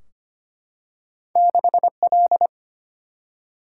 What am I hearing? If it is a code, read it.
Morse code, heard as a keyed tone, about 25 wpm, 700 Hz. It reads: 6L
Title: Single-Letter Single-Number 25wpm